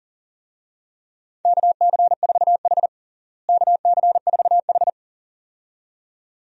Morse code, heard as a keyed tone, about 40 wpm, 700 Hz. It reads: KC4H KC4H
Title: USA Call Signs - 4 Characters - Repeated 2x - 40wpm